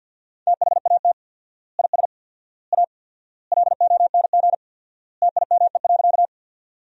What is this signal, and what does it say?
Morse code, heard as a keyed tone, about 48 wpm, 700 Hz. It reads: THAT IS A LONG TIME.
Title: 4x Word-Spacing - Sentences from Top 100 Words - 50wpm